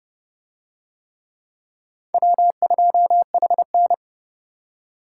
Morse code, heard as a keyed tone, about 29 wpm, 700 Hz. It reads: W2HN